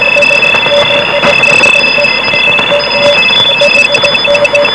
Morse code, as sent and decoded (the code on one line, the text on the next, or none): none